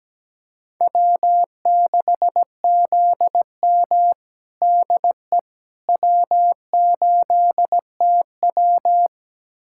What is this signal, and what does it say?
Morse code, heard as a keyed tone, about 17 wpm, 700 Hz. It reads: W6ZM DE W8TW
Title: USA Callsign DE Callsign - 4 Characters - 17wpm